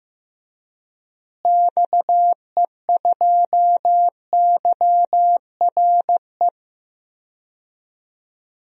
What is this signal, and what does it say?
Morse code, heard as a keyed tone, about 15 wpm, 700 Hz. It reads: XE2YRE
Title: DXCC-100 Call Signs - 6 Characters - Repeated 2x - 15wpm